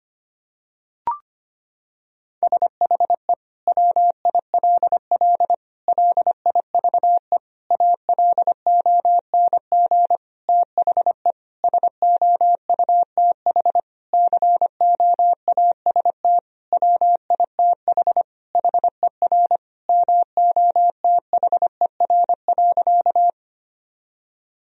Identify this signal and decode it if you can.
Morse code, heard as a keyed tone, about 25 wpm, 700 Hz. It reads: SHE WILL LIVE ALONG THE SOUTH COAST WITH HER MOTHER.
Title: Sentences from Top 1000 Words - Review Part 4 - 25wpm